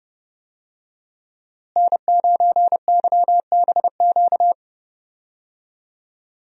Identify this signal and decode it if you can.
Morse code, heard as a keyed tone, about 30 wpm, 700 Hz. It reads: N9YBQ
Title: USA Call Signs - 5 Characters - Repeated 3x - 30wpm